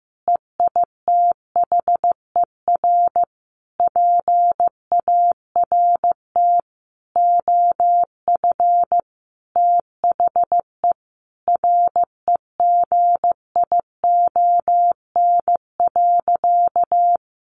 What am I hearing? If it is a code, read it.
Morse code, heard as a keyed tone, about 15 wpm, 700 Hz. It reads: EITHER PART OF THE REGION.
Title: Sentences from Top 1500 Words 15wpm